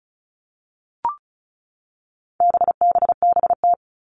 Morse code, heard as a keyed tone, about 34 wpm, 700 Hz. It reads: BBBT
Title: All Cut Numbers - 4-Digit 35wpm